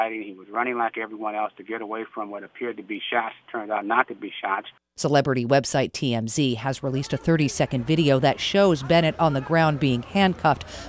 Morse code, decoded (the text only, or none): none